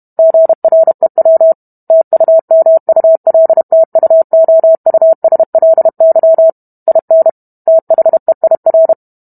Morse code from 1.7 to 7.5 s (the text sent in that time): TUMULTUOUSLY IN